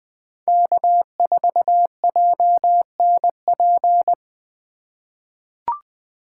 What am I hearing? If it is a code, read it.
Morse code, heard as a keyed tone, about 20 wpm, 700 Hz. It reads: K4JNP
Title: USA Call Signs - 5 Characters - 3x Character Spacing - 20wpm